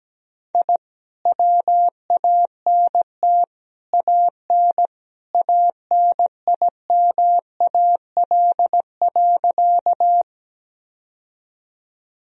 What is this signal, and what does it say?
Morse code, heard as a keyed tone, about 17 wpm, 700 Hz. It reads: I WANT AN ANIMAL.
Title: Sentences from Top 1000 Words - Review Part 1 - 17wpm